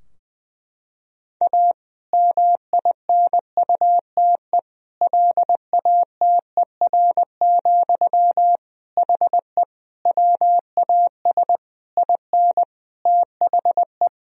Morse code, heard as a keyed tone, about 20 wpm, 700 Hz. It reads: A MINUTE LATER, HE WAS IN THE
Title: Sentences from Top 1000 Words - Review Part 4 - 20wpm